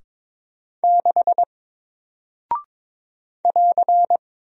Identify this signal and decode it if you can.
Morse code, heard as a keyed tone, about 22 wpm, 700 Hz. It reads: 6 <AR>
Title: Single Character (Letter:Number:Punctuation:Prosign) Alt 22wpm